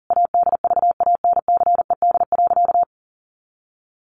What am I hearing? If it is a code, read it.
Morse code, heard as a keyed tone, about 40 wpm, 700 Hz. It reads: ADVANCED.
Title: Sentences from Top 1200 Words 40wpm